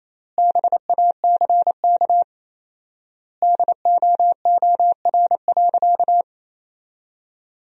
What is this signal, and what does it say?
Morse code, heard as a keyed tone, about 28 wpm, 700 Hz. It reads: BACK DOOR.
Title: 4x Word-Spacing - Sentences from Top 300 Words - 28wpm